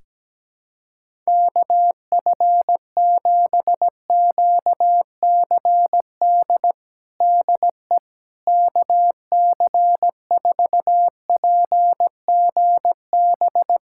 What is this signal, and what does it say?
Morse code, heard as a keyed tone, about 17 wpm, 700 Hz. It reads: KF7QCD DE KC4PGB
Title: USA Callsign DE Callsign 17wpm